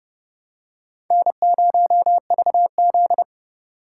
Morse code, heard as a keyed tone, about 30 wpm, 700 Hz. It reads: N0VZ